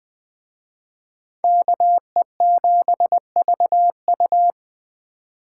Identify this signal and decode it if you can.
Morse code, heard as a keyed tone, about 20 wpm, 700 Hz. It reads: KE7VU
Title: Speed Racing - USA 2x2 Call Signs 17wpm